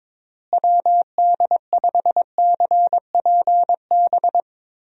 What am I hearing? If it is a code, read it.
Morse code, heard as a keyed tone, about 22 wpm, 700 Hz. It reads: WD5CPB
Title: USA Call Signs - 6 Characters - Repeated 2x - 22wpm